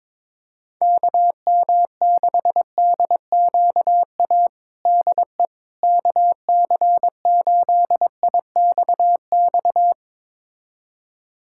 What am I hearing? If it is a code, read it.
Morse code, heard as a keyed tone, about 22 wpm, 700 Hz. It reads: KM6DQA DE KC8IXX
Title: USA Callsign DE Callsign 22wpm